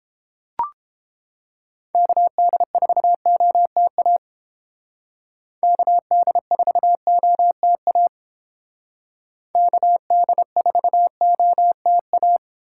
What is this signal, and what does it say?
Morse code, heard as a keyed tone, about 29 wpm, 700 Hz. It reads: KD4OTA KD4OTA KD4OTA